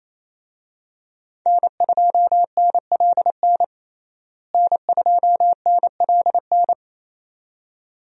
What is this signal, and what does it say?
Morse code, heard as a keyed tone, about 28 wpm, 700 Hz. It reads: N2NLN N2NLN
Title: USA Call Signs - 5 Characters - Repeated 2x - 28wpm